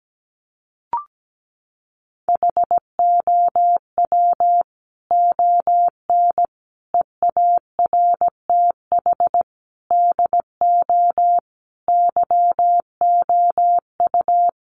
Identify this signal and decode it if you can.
Morse code, heard as a keyed tone, about 17 wpm, 700 Hz. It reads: HOW ON EARTH DO YOU
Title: Sentences from Top 200 Words 17wpm